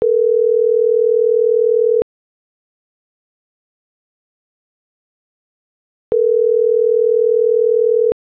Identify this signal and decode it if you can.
Ringback tone (North America), 440 + 480 Hz.
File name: CENTRAL GENERAL / out ricardo.diaz